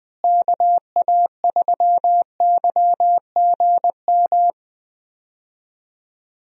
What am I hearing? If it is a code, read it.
Morse code, heard as a keyed tone, about 20 wpm, 700 Hz. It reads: KA3YGM
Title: USA Call Signs - 6 Characters - Repeated 3x - 20wpm